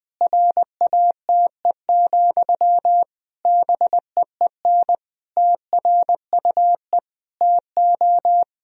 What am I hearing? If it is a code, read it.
Morse code, heard as a keyed tone, about 20 wpm, 700 Hz. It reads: RATE, BEEN TRUE TO
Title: Sentences from Top 1000 Words - Review Part 4 - 20wpm